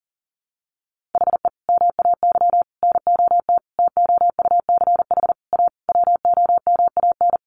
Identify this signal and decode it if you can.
Morse code, heard as a keyed tone, about 40 wpm, 700 Hz. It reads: HE MAY NOT TOUCH A WOMAN
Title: Sentences from Top 1500 Words 40wpm